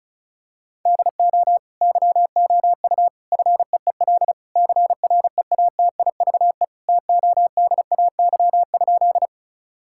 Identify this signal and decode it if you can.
Morse code, heard as a keyed tone, about 34 wpm, 700 Hz. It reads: DO YOU FEEL CREATIVE TODAY?
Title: Sentences from Top 1000 Words - Review Part 5 - 35wpm